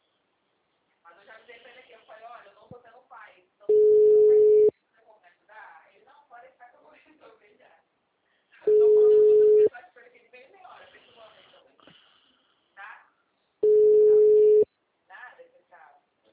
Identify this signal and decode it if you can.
Ringback tone (Europe, 425 Hz), with speech.